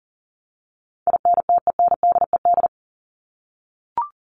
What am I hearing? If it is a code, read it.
Morse code, heard as a keyed tone, about 38 wpm, 700 Hz. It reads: INTENDED
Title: ED Suffix - Top 100 Words with Base Words 40wpm